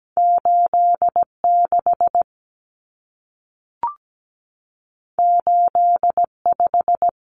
Morse code, heard as a keyed tone, about 17 wpm, 700 Hz. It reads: 86 85
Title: Numbers - 2-Digit 17wpm